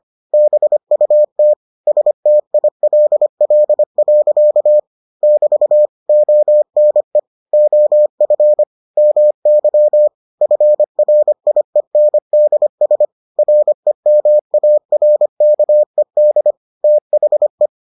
Morse code, heard as a keyed tone, about 25 wpm, 600 Hz. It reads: BUT STILL. <BT> ONE OF MY FRIENDS REMARKED THE